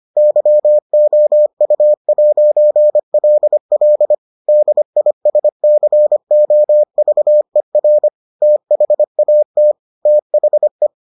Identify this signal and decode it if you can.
Morse code, heard as a keyed tone, about 25 wpm, 600 Hz. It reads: YOU'LL DISCOVER THAT THE